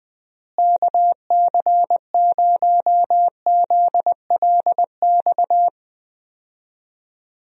Morse code, heard as a keyed tone, about 20 wpm, 700 Hz. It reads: KC0ZLX
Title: USA Call Signs - 6 Characters - Repeated 2x - 20wpm